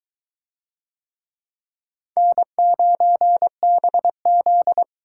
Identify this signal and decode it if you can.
Morse code, heard as a keyed tone, about 20 wpm, 700 Hz. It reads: N9BZ